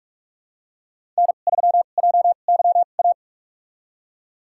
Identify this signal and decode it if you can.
Morse code, heard as a keyed tone, about 46 wpm, spaced as for 32 wpm, 700 Hz. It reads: N3JYA
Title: USA Call Signs - 2x Character Spacing - 45wpm